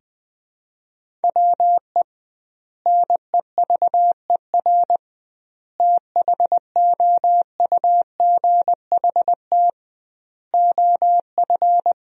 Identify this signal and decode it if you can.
Morse code, heard as a keyed tone, about 20 wpm, 700 Hz. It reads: WE NEVER THOUGHT OF
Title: 2x Word-Spacing - Sentences from Top 300 Words - 20wpm